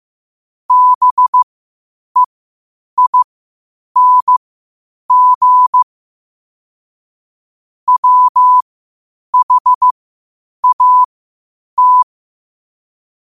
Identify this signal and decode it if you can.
Morse code, heard as a keyed tone, about 15 wpm, spaced as for 8 wpm, 1000 Hz. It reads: BEING WHAT
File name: qotdfiles / 8wpm